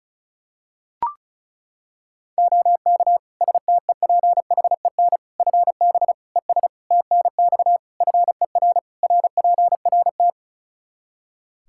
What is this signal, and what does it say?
Morse code, heard as a keyed tone, about 34 wpm, 700 Hz. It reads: OK STEPHEN FB ES TNX FER RPRT
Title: Practice QSO - Exchange 2 - Info - 35wpm